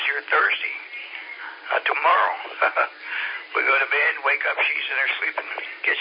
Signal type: Emergency Alert System (SAME) tones